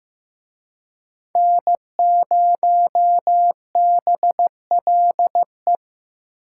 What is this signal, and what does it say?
Morse code, heard as a keyed tone, about 18 wpm, 700 Hz. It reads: N0BLE
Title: Speed Racing - USA Call Signs 15wpm